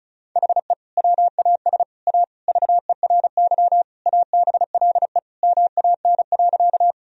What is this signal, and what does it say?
Morse code, heard as a keyed tone, about 34 wpm, 700 Hz. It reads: HE WAS A VERY ABLE MAN.
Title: Sentences from Top 1000 Words - Review Part 2 - 35wpm